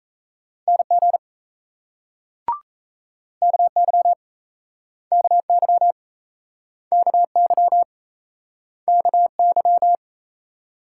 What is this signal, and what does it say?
Morse code, heard as a keyed tone, about 39 wpm, 700 Hz. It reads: NG KY KY KY KY